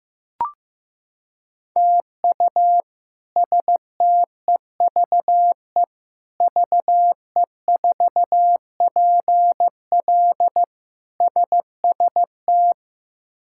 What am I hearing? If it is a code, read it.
Morse code, heard as a keyed tone, about 16 wpm, 700 Hz. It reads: TU STEVE VE4PL SST
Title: K1USN SST QSO Practice - 15wpm